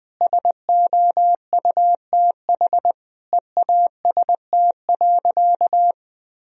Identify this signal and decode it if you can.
Morse code, heard as a keyed tone, about 21 wpm, 700 Hz. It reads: SOUTH EAST.
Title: Sentences from Top 1000 Words - Review Part 3 - 20wpm